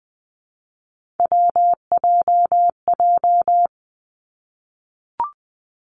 Morse code, heard as a keyed tone, about 20 wpm, 700 Hz. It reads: WJJ
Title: Three Challenging Letter-Number Combinations 20wpm